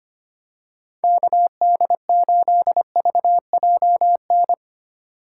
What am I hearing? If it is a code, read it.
Morse code, heard as a keyed tone, about 25 wpm, 700 Hz. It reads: KD8VJN
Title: USA Call Signs - 6 Characters - 2x Character Spacing - 25wpm